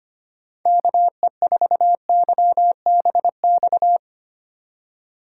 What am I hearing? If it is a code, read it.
Morse code, heard as a keyed tone, about 25 wpm, 700 Hz. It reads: KE4YBX
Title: USA Call Signs - 6 Characters - 4x Character Spacing - 25wpm